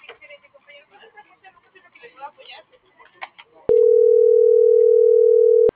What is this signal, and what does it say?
Ringback tone (North America), 440 + 480 Hz.
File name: CENTRAL GENERAL / out karla.sky